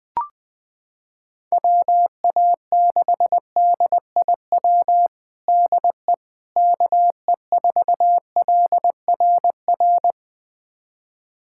Morse code, heard as a keyed tone, about 21 wpm, 700 Hz. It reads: WA6DIW DE KE4LRR
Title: USA Callsign DE Callsign 20wpm